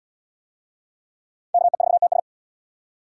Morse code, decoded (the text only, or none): KE4TS